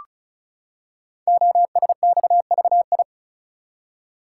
Morse code, heard as a keyed tone, about 36 wpm, 690 Hz. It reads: OSXVI
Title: Code Groups - 5 Letters - 35wpm